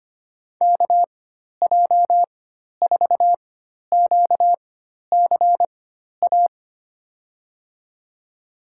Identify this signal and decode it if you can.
Morse code, heard as a keyed tone, about 25 wpm, spaced as for 12 wpm, 700 Hz. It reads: KJ4QCA
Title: USA Call Signs - 4x Character Spacing - 25wpm